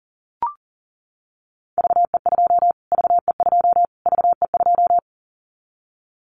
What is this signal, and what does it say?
Morse code, heard as a keyed tone, about 40 wpm, 700 Hz. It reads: VE2 VE2 VE2
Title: Callsign Prefixes for Top 25 DXCC - Repeated 3x - 40wpm